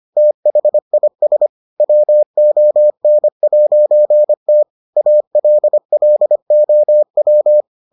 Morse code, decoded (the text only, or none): THIS WON'T ALLOW